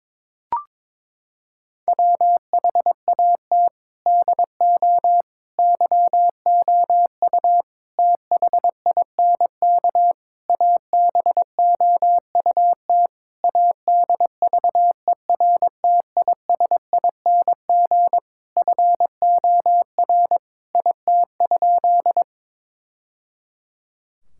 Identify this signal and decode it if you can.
Morse code, heard as a keyed tone, about 22 wpm, 700 Hz. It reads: WHAT DO YOU THINK ABOUT ADVERTISING FOR IT?
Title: Sentences from Top 1000 Words - Review Part 6 - 22wpm